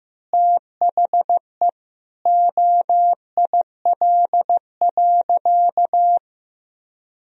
Morse code, heard as a keyed tone, about 15 wpm, 700 Hz. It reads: THE OIL.
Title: Sentences from Top 1000 Words - Review Part 4 - 15wpm